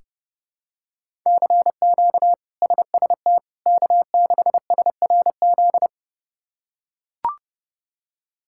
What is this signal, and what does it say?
Morse code, heard as a keyed tone, about 31 wpm, 700 Hz. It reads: CQ SST K6SRZ
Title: K1USN SST QSO Practice - 30wpm